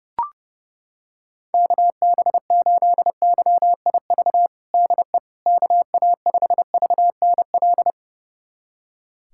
Morse code, heard as a keyed tone, about 30 wpm, 700 Hz. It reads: KB8YIV DE KA5VNL
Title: USA Callsign DE Callsign - 6 Characters - 30wpm